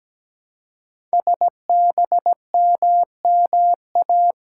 Morse code, heard as a keyed tone, about 17 wpm, 700 Hz. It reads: SBMMA
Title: Code Groups - 5 Letters - 17wpm